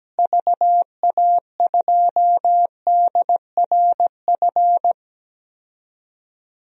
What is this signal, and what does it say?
Morse code, heard as a keyed tone, about 17 wpm, 700 Hz. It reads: VA2DRF
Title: DXCC-100 Call Signs - 6 Characters - Repeated 3x - 17wpm